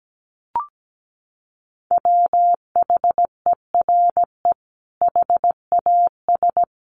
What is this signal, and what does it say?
Morse code, heard as a keyed tone, about 17 wpm, 700 Hz. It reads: WHERE HAS
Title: Sentences from Top 300 Words 17wpm